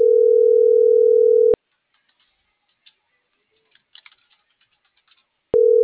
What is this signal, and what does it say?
Ringback tone (North America), 440 + 480 Hz.